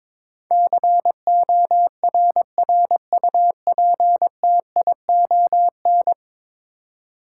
Morse code, heard as a keyed tone, about 22 wpm, 700 Hz. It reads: CORRUPTION
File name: practice / 22wpm / 10 Letter Words - First 3 Letters Given - 22wpm